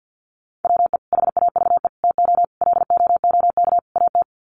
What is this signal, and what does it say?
Morse code, heard as a keyed tone, about 50 wpm, 700 Hz. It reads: WE HAVE TO LOOK AT
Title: Sets of 5 Words Encore2 50wpm